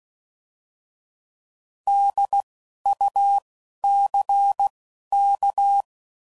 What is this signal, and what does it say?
Morse code, heard as a keyed tone, about 16 wpm, spaced as for 11 wpm, 800 Hz. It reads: DUCK
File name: Prefixes / By First Letter / Ds